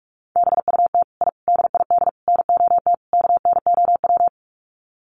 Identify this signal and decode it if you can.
Morse code, heard as a keyed tone, about 44 wpm, 700 Hz. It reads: BUT I DID NOT KNOW
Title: Sets of 5 Words Encore 45wpm